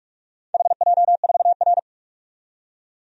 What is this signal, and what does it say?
Morse code, heard as a keyed tone, about 48 wpm, 700 Hz. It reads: HJ4R
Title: DXCC-100 Call Signs - 4 Characters - 45wpm